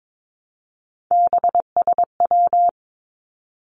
Morse code, heard as a keyed tone, about 22 wpm, 700 Hz. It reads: BSW
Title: Three Challenging Letter-Number Combinations 22wpm